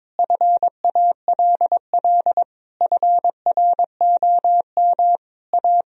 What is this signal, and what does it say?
Morse code, heard as a keyed tone, about 21 wpm, 700 Hz. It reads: FALL FROM A
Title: Sentences from Top 1000 Words - Review Part 2 - 22wpm